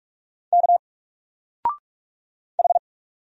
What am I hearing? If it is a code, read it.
Morse code, heard as a keyed tone, about 46 wpm, 700 Hz. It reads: K H
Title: Single Letter-Number 45wpm